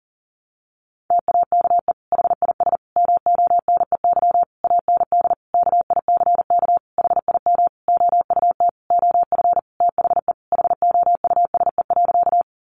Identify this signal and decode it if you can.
Morse code, heard as a keyed tone, about 40 wpm, 700 Hz. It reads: TAKE HIS MONEY AND KICK HIM OUT OF THE HOUSE.
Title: Sentences from Top 1300 Words 40wpm